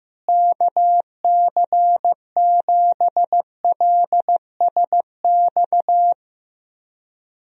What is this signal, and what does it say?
Morse code, heard as a keyed tone, about 14 wpm, 700 Hz. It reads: KC7LSX
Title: USA Call Signs - 6 Characters - Repeated 3x - 15wpm